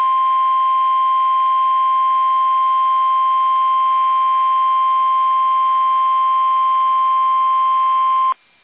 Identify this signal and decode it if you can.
Emergency Alert System (SAME) tones (the 1050 Hz attention tone). It is heard from 0.6 to 8.1 s.
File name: eas / KUSO / KUSO